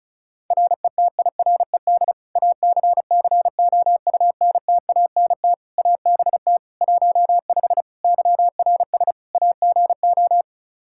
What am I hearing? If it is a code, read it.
Morse code, heard as a keyed tone, about 36 wpm, 700 Hz. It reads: RETIRED ACCOUNTANT ABT 15 YRS AGO
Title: Practice QSO - Exchange 3 - More Info - 35wpm